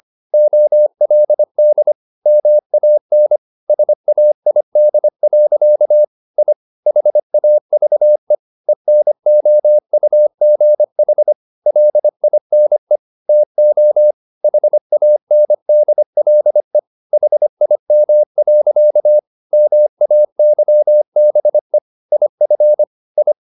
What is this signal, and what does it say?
Morse code, heard as a keyed tone, about 25 wpm, 600 Hz. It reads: OLD MAN SAID. I HAVE ENOUGH LINE TO HANDLE HIM. MAYBE IF I
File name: the old man and the sea part 1 25wpm